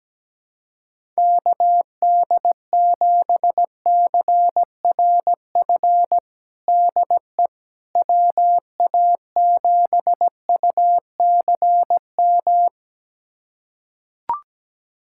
Morse code, heard as a keyed tone, about 17 wpm, 700 Hz. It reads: KD7CRF DE WA7UCM
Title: USA Callsign DE Callsign - 6 Characters - 17wpm